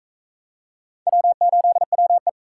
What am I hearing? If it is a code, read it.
Morse code, heard as a keyed tone, about 39 wpm, 700 Hz. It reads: W8WE